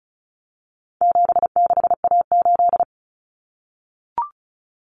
Morse code, heard as a keyed tone, about 34 wpm, 700 Hz. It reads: 76A8
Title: Common Cut Numbers - 4-Digit 35wpm